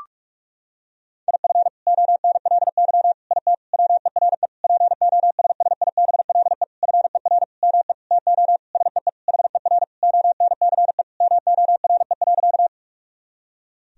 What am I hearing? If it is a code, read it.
Morse code, heard as a keyed tone, about 46 wpm, 700 Hz. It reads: IF ONLY IT WERE POSSIBLE FER ME TO SEE HER ONCE MORE.
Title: Sentences from Top 500 Words and Abbreviations 45wpm